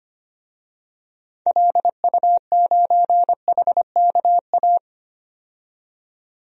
Morse code, heard as a keyed tone, about 25 wpm, 700 Hz. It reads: LU9HKA
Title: DXCC-100 Call Signs - 6 Characters - 25wpm